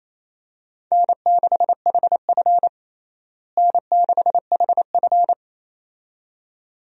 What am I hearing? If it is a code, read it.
Morse code, heard as a keyed tone, about 27 wpm, 700 Hz. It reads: N6HF N6HF
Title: USA Call Signs - 4 Characters - Repeated 2x - 28wpm